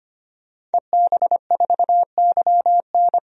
Morse code, heard as a keyed tone, about 25 wpm, 700 Hz. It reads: EB4YN
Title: DXCC-100 Call Signs - 5 Characters - 25wpm